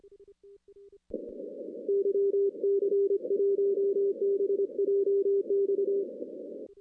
Morse code, heard as a keyed tone, about 25 wpm, 400 Hz. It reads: YC1BJX
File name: 2016 arrl dx cw / 2016-ARRL-DX-CW-K5TR- - 0-25